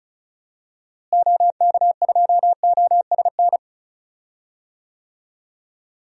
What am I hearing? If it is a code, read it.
Morse code, heard as a keyed tone, about 36 wpm, 700 Hz. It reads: OK2OSN
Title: DXCC-100 Call Signs - 6 Characters - Repeated 3x - 35wpm